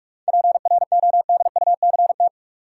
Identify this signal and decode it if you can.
Morse code, heard as a keyed tone, about 47 wpm, 700 Hz. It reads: PRODUCT
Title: Sentences from Top 1000 Words - Review Part 2 - New Word Drill - 45wpm